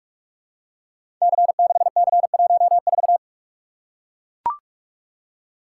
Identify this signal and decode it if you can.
Morse code, heard as a keyed tone, about 44 wpm, 700 Hz. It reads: C6C14
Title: Code Groups - 5 Letter:Numbers - 45wpm